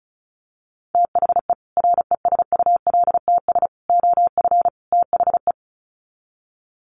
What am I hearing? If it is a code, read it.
Morse code, heard as a keyed tone, about 34 wpm, 700 Hz. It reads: THE RESULTS OF THE
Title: Sets of 4 Words 35wpm